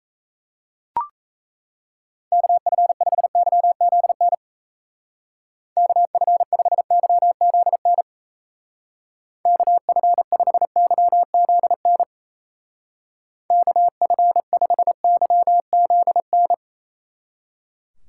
Morse code, heard as a keyed tone, about 36 wpm, 700 Hz. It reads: KF5YZN KF5YZN KF5YZN KF5YZN